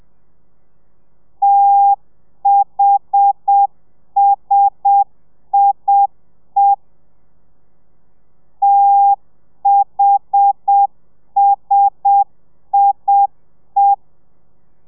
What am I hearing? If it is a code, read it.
Morse code, heard as a keyed tone, about 7 wpm, 800 Hz. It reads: THSIE THSIE
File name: telegrafia / i2tfj / Lez1 30